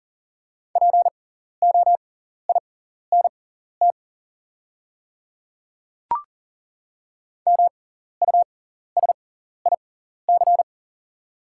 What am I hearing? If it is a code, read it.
Morse code, heard as a keyed tone, about 40 wpm, spaced as for 14 wpm, 700 Hz. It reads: POINT MUSIC
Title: 6x Character Spacing - Top 300 Words - 40wpm